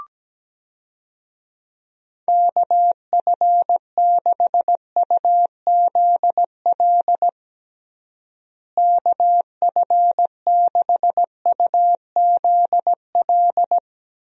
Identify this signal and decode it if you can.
Morse code, heard as a keyed tone, about 17 wpm, 700 Hz. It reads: KF6UZL KF6UZL
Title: USA Call Signs - 6 Characters - Repeated 2x - 17wpm